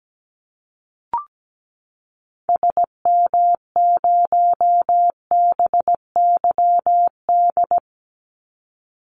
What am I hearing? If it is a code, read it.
Morse code, heard as a keyed tone, about 17 wpm, 700 Hz. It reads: SM0BYD
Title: Call Signs Repeated 2x - 17wpm